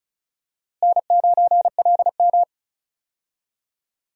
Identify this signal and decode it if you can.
Morse code, heard as a keyed tone, about 36 wpm, 700 Hz. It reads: N9LM
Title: USA Call Signs - 4 Characters - Repeated 3x - 35wpm